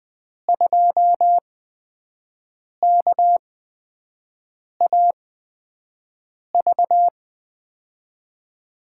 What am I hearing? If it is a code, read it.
Morse code, heard as a keyed tone, about 20 wpm, spaced as for 5 wpm, 700 Hz. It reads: 2KAV
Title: USA Call Signs - 5 Characters - 8x Character Spacing - 20wpm